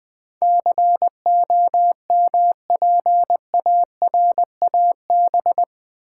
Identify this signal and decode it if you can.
Morse code, heard as a keyed tone, about 20 wpm, 700 Hz. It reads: COMPARAB
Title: Build A Word - 4 Letters At A Time - 10-Letter Words - 20wpm